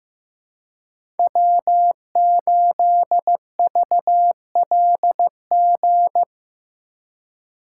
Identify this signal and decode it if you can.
Morse code, heard as a keyed tone, about 15 wpm, 700 Hz. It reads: W8VLG
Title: USA Call Signs - 3x Character Spacing - 15wpm